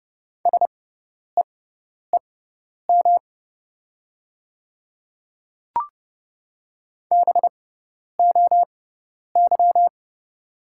Morse code, heard as a keyed tone, about 30 wpm, spaced as for 10 wpm, 690 Hz. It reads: SEEM BOY
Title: 6x Character Spacing - Top 300 Words - 30wpm